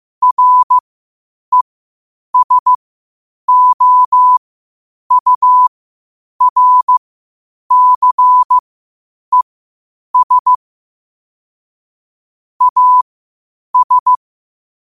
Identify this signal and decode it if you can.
Morse code, heard as a keyed tone, about 15 wpm, spaced as for 8 wpm, 1000 Hz. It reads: RESOURCES AS
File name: qotdfiles / 8wpm